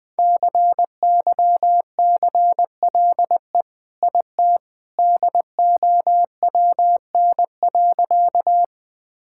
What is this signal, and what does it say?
Morse code, heard as a keyed tone, about 21 wpm, 700 Hz. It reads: CYCLE IT DOWN.
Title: Sentences from Top 1000 Words - Review Part 6 - 20wpm